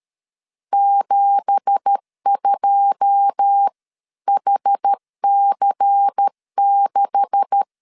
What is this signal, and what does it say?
Morse code, heard as a keyed tone, about 13 wpm, 800 Hz. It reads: 72HC6
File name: LEZIONI / 10 SIGNS / 10D COMA 12wpm 2544